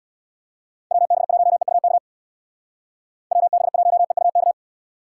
Morse code, heard as a keyed tone, about 63 wpm, 700 Hz. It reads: WB1EUK WB1EUK